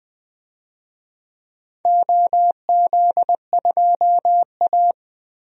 Morse code, heard as a keyed tone, about 20 wpm, 700 Hz. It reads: OZ2A
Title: DXCC-100 Call Signs - 4 Characters - Repeated 3x - 20wpm